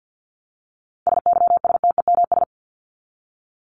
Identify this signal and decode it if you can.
Morse code, heard as a keyed tone, about 48 wpm, 700 Hz. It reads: SYSTEMS
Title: ES:S Suffix - Top 100 Words with Base Words 50wpm